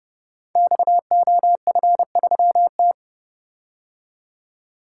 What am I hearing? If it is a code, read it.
Morse code, heard as a keyed tone, about 30 wpm, 700 Hz. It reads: XOF3T
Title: Code Groups - 5 Letter:Numbers - 30wpm